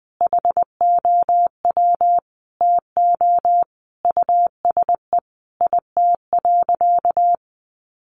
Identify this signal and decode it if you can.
Morse code, heard as a keyed tone, about 20 wpm, 700 Hz. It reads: HOW TO USE IT.
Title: Sentences from Top 500 Words 20wpm